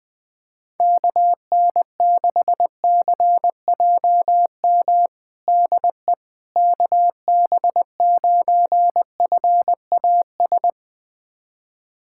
Morse code, heard as a keyed tone, about 20 wpm, 700 Hz. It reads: KN6CJM DE KB9FAS
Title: USA Callsign DE Callsign 20wpm